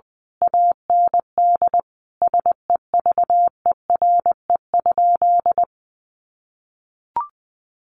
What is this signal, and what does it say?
Morse code, heard as a keyed tone, about 19 wpm, 700 Hz. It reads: AND SEVERE?
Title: Sentences from Top 1400 Words 20wpm